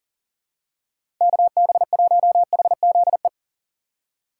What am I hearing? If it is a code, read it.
Morse code, heard as a keyed tone, about 40 wpm, 700 Hz. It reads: KB1HZE
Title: USA Call Signs - 6 Characters - 3x Character Spacing - 40wpm